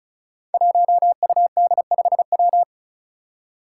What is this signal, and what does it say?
Morse code, heard as a keyed tone, about 36 wpm, 700 Hz. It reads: 1UD5W
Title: Five Challenging Letter-Number Combinations 35wpm